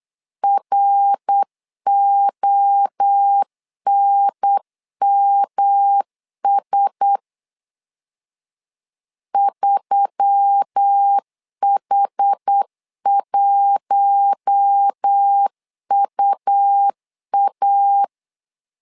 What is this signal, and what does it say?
Morse code, heard as a keyed tone, about 8 wpm, 800 Hz. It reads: RONMS 3H1UA
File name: LEZIONI / 4 Q1S3 8wpm 3037